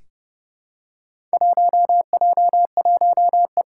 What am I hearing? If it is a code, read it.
Morse code, heard as a keyed tone, about 30 wpm, 700 Hz. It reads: 1J1E
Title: Four Challenging Letter-Number Combinations 30wpm